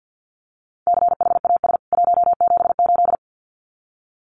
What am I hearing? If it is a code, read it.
Morse code, heard as a keyed tone, about 50 wpm, 700 Hz. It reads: CHAS 178
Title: CWT Exchange 50wpm